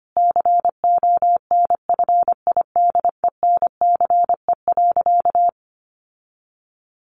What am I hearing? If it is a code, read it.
Morse code, heard as a keyed tone, about 25 wpm, 700 Hz. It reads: CONFIDENCE.
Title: Sentences from Top 1500 Words 25wpm